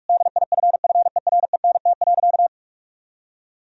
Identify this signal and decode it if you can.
Morse code, heard as a keyed tone, about 46 wpm, 700 Hz. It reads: DIFFERENT.
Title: Sentences from Top 1000 Words - Review Part 3 - 45wpm